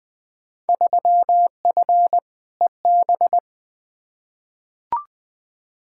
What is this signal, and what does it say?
Morse code, heard as a keyed tone, about 20 wpm, 700 Hz. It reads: 3F EB
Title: Field Day Exchanges - 20wpm